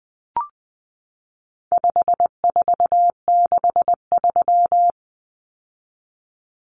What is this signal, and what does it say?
Morse code, heard as a keyed tone, about 20 wpm, 700 Hz. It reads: 5463
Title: Numbers - 4-Digit 20wpm